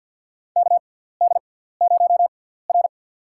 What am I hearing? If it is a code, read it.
Morse code, heard as a keyed tone, about 53 wpm, spaced as for 17 wpm, 700 Hz. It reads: KD0R
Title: USA Call Signs - 6x Character Spacing - 50wpm